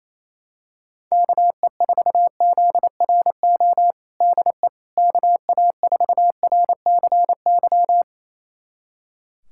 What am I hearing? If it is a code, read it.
Morse code, heard as a keyed tone, about 28 wpm, 700 Hz. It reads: KE4ZRO DE KA4RCY
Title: USA Callsign DE Callsign 28wpm